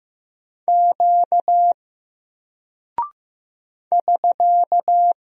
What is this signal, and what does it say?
Morse code, heard as a keyed tone, about 15 wpm, 700 Hz. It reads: Q <SK>
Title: Single Character (Letter:Number:Punctuation:Prosign) Alt 15wpm